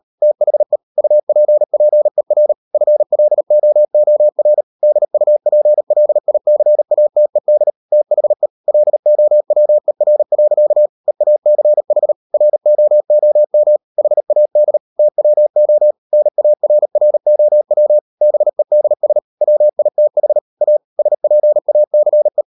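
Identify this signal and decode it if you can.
Morse code, heard as a keyed tone, about 37 wpm, 600 Hz. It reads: THE UPPER FLOOR DUPLICATED THE LOWER. EACH ROOM HAD TWO NARROW BEDS WITH A SPACE
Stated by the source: Morse code practice at 38 WPM, sending Ken Follett's novel Never in English, from CW med Gnister.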